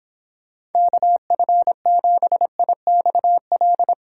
Morse code, heard as a keyed tone, about 31 wpm, 700 Hz. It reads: KF7IXL